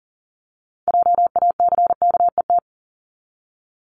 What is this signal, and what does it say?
Morse code, heard as a keyed tone, about 40 wpm, 700 Hz. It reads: JACKET
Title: Top 1751-2000 Words 40wpm